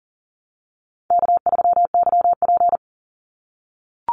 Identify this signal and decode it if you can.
Morse code, heard as a keyed tone, about 40 wpm, 700 Hz. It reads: K3YP
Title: Call Signs - 40wpm